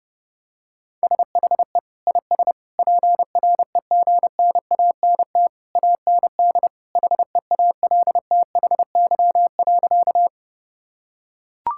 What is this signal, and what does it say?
Morse code, heard as a keyed tone, about 29 wpm, 700 Hz. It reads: SHE IS PREGNANT AND HEALTHY.
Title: Sentences from Top 1000 Words - Review Part 6 - 30wpm